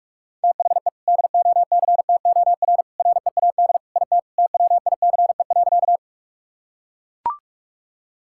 Morse code, heard as a keyed tone, about 41 wpm, 700 Hz. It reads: THE DOCTOR READ IT TWICE.
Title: Sentences from Top 1600 Words 45wpm